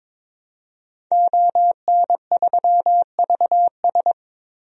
Morse code, heard as a keyed tone, about 21 wpm, 700 Hz. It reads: ON3VS
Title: DXCC-100 Call Signs - 5 Characters - Repeated 3x - 22wpm